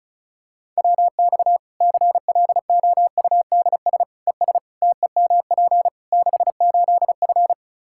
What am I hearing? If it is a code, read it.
Morse code, heard as a keyed tone, about 36 wpm, 700 Hz. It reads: WX CLOUDS ES TEMP 68F
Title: Practice QSO - Exchange 2 - Info - 35wpm